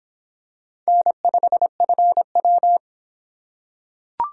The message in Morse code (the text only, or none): N5FW